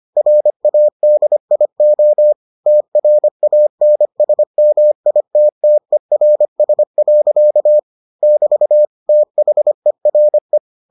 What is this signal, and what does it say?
Morse code, heard as a keyed tone, about 25 wpm, 600 Hz. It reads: RADIO TRANSMITTERS. <BT> THERE